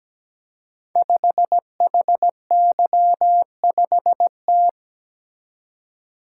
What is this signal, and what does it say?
Morse code, heard as a keyed tone, about 17 wpm, 700 Hz. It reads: 5HY5T
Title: Code Groups - 5 Letter:Numbers - 17wpm